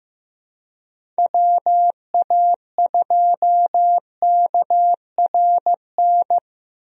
Morse code, heard as a keyed tone, about 15 wpm, 700 Hz. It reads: WA2KRN
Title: USA Call Signs - 6 Characters - Repeated 2x - 15wpm